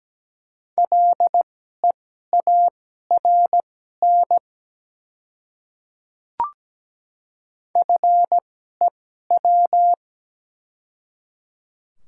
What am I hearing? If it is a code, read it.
Morse code, heard as a keyed tone, about 17 wpm, spaced as for 12 wpm, 690 Hz. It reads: LEARN FEW
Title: 2x Character Spacing - Top 300 Words - 17wpm